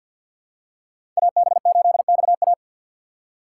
Morse code, heard as a keyed tone, about 51 wpm, 700 Hz. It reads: AB8XA
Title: USA Call Signs - 5 Characters - 2x Character Spacing - 50wpm